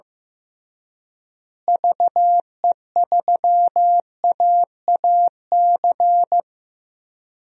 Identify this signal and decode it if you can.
Morse code, heard as a keyed tone, about 15 wpm, 700 Hz. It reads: VE3AAC
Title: DXCC-100 Call Signs - 6 Characters - Repeated 3x - 15wpm